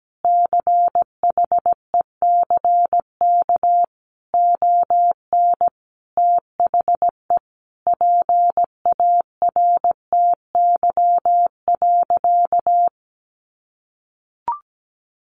Morse code, heard as a keyed tone, about 17 wpm, 700 Hz. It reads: CHECK ON THE PARTY.
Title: Sentences from Top 1400 Words 17wpm